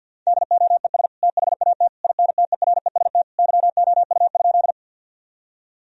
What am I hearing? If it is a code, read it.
Morse code, heard as a keyed tone, about 50 wpm, 700 Hz. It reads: DOES THAT INTEREST YOU?
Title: Sentences from Top 1000 Words - Review Part 2 - 50wpm